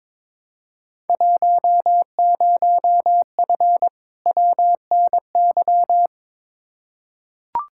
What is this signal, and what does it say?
Morse code, heard as a keyed tone, about 22 wpm, 700 Hz. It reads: 10F WNY
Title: Field Day Exchanges Encore - 22wpm